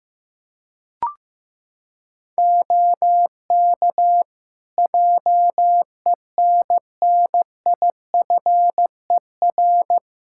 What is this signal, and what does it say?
Morse code, heard as a keyed tone, about 16 wpm, 700 Hz. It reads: OK JENNIFER
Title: Practice QSO - Exchange 3 - More Info - 15wpm